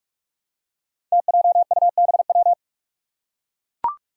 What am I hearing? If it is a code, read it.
Morse code, heard as a keyed tone, about 44 wpm, 700 Hz. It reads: TJUBW
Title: Five Challenging Letter-Number Combinations 45wpm